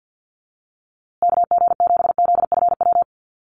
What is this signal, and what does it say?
Morse code, heard as a keyed tone, about 50 wpm, 700 Hz. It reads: KG7ZFW
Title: USA Call Signs Repeated 2x 50wpm